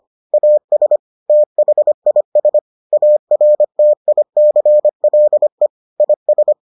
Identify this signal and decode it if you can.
Morse code, heard as a keyed tone, about 25 wpm, 600 Hz. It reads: AS THIS ARTICLE IS